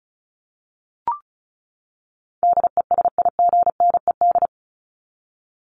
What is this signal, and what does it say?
Morse code, heard as a keyed tone, about 34 wpm, 700 Hz. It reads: DESIGNED
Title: ED Suffix - Top 100 Words with Base Words 35wpm